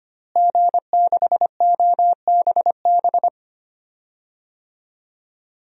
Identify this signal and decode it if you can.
Morse code, heard as a keyed tone, about 25 wpm, 700 Hz. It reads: G6OBB
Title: DXCC-100 Call Signs - 5 Characters - Repeated 2x - 25wpm